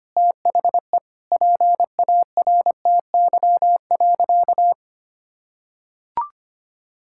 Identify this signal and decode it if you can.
Morse code, heard as a keyed tone, about 25 wpm, 690 Hz. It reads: THE PARTY.
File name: practice / 25wpm / Sentences with 2-Word Phrases - 25wpm